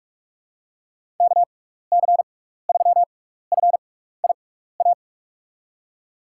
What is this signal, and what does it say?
Morse code, heard as a keyed tone, about 44 wpm, spaced as for 15 wpm, 700 Hz. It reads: KC3FIA
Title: USA Call Signs - 6 Characters - 6x Character Spacing - 45wpm